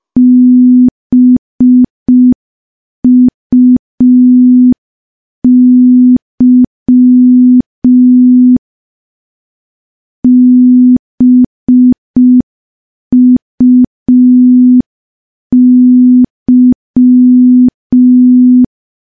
Morse code, heard as a keyed tone, about 5 wpm, 250 Hz. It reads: BUY BUY
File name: morse / morse-training-05WPM